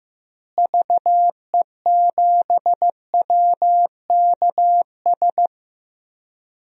Morse code, heard as a keyed tone, about 15 wpm, 700 Hz. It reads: VE7WKS
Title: DXCC-100 Call Signs - 6 Characters - Repeated 3x - 15wpm